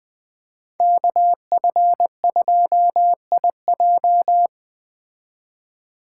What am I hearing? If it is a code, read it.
Morse code, heard as a keyed tone, about 20 wpm, 700 Hz. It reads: KF2IJ
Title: Speed Racing - USA 2x2 Call Signs 17wpm